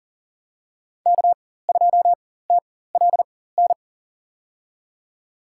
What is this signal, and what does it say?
Morse code, heard as a keyed tone, about 40 wpm, spaced as for 19 wpm, 700 Hz. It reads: K2TLN
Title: USA Call Signs - 5 Characters - 4x Character Spacing - 40wpm